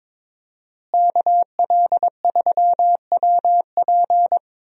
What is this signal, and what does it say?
Morse code, heard as a keyed tone, about 22 wpm, 700 Hz. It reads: KL3WP
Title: USA Call Signs - 5 Characters - Repeated 2x - 22wpm